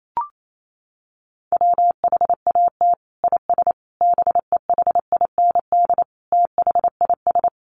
Morse code, heard as a keyed tone, about 28 wpm, 700 Hz. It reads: WHAT IS BEHIND THIS
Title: Sentences from Top 1100 Words 28wpm